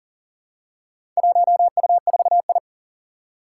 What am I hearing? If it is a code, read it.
Morse code, heard as a keyed tone, about 40 wpm, 700 Hz. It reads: 1U4I
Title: Four Challenging Letter-Number Combinations 40wpm